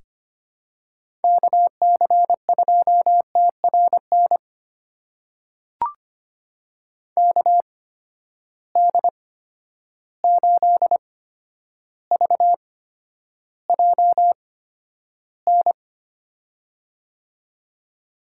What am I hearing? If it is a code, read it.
Morse code, heard as a keyed tone, about 25 wpm, spaced as for 11 wpm, 700 Hz. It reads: KC2TRN KD8VJN
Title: USA Call Signs - 6 Characters - 8x Character Spacing - 25wpm